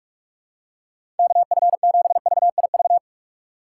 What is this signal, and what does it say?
Morse code, heard as a keyed tone, about 41 wpm, 700 Hz. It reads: KF7VIV
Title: USA Call Signs - 6 Characters - 8x Character Spacing - 45wpm